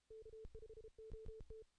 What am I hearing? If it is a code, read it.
Morse code, heard as a keyed tone, about 33 wpm, 440 Hz. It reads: K5OT
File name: 2016 JAN CW NAQP / 2016 Jan naqp k5ot - 0-8